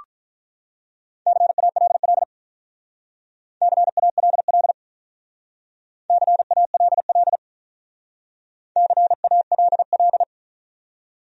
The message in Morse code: CALL CALL CALL CALL

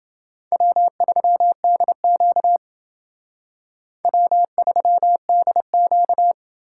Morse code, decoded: W3DQ W3DQ

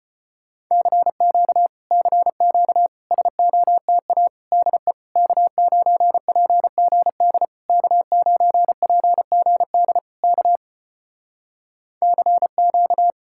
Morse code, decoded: CQ CQ SOTA DE K9PGD K9PGD K CQ

